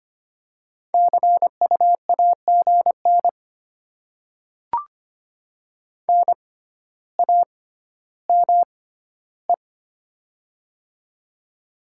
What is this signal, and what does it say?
Morse code, heard as a keyed tone, about 25 wpm, spaced as for 13 wpm, 700 Hz. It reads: CU AGN NAME
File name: Top 20 QSO Elements - 6x Character Spacing - 25wpm